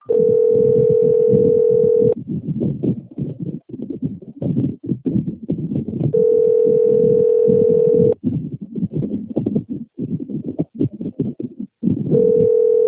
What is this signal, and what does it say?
Telephone call progress tone (North America), heard as a telephone network's tone, about 440, 480 Hz, with noise. Ringback tone.